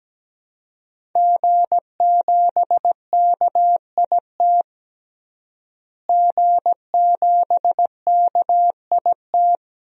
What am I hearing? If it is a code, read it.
Morse code, heard as a keyed tone, about 17 wpm, 700 Hz. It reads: G7KIT G7KIT
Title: DXCC-100 Call Signs - 5 Characters - Repeated 2x - 17wpm